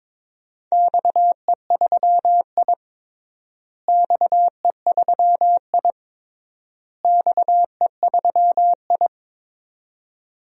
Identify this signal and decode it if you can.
Morse code, heard as a keyed tone, about 23 wpm, 700 Hz. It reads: XE3I XE3I XE3I
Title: DXCC-100 Call Signs - 4 Characters - Repeated 3x - 22wpm